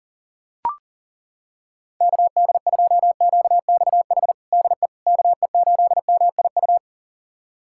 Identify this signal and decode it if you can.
Morse code, heard as a keyed tone, about 40 wpm, 700 Hz. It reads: KD2QXH DE KE8MIU
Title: USA Callsign DE Callsign - 6 Characters - 40wpm